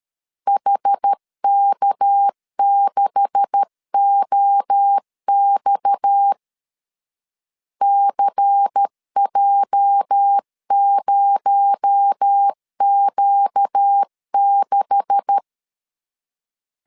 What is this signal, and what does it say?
Morse code, heard as a keyed tone, about 13 wpm, 800 Hz. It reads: HK6OX CJ0Q6
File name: LEZIONI / 9 286 12wpm 2305